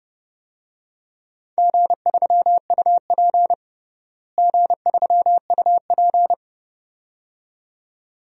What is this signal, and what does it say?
Morse code, heard as a keyed tone, about 30 wpm, 700 Hz. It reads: G3UP G3UP
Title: DXCC-100 Call Signs - 4 Characters - Repeated 2x - 30wpm